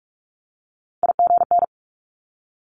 Morse code, heard as a keyed tone, about 46 wpm, 700 Hz. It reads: IGN
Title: Multi-letter Phonemes 45wpm